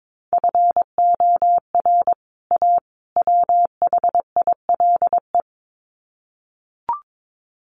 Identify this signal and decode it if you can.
Morse code, heard as a keyed tone, about 22 wpm, 700 Hz. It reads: FOR A WHILE
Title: Sets of 3 Words 22wpm